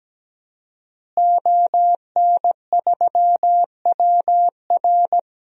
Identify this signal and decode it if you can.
Morse code, heard as a keyed tone, about 17 wpm, 700 Hz. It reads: ON3WR
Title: DXCC-100 Call Signs - 5 Characters - Repeated 3x - 17wpm